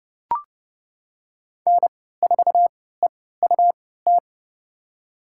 Morse code, heard as a keyed tone, about 30 wpm, spaced as for 17 wpm, 700 Hz. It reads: N4EUT
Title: USA Call Signs - 5 Characters - 3x Character Spacing - 30wpm